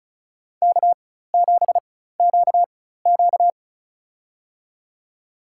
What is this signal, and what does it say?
Morse code, heard as a keyed tone, about 34 wpm, spaced as for 16 wpm, 700 Hz. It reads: K7QQ
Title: USA Call Signs - 4 Characters - 4x Character Spacing - 35wpm